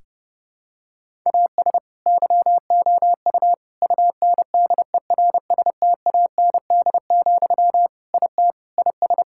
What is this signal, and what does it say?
Morse code, heard as a keyed tone, about 30 wpm, 700 Hz. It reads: AS YOU UNDERSTAND, IT IS
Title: Sentences from Top 1000 Words - Review Part 5 - 30wpm